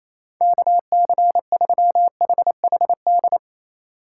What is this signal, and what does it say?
Morse code, heard as a keyed tone, about 28 wpm, 700 Hz. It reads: KC3HHD
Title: USA Call Signs - 6 Characters - 6x Character Spacing - 28wpm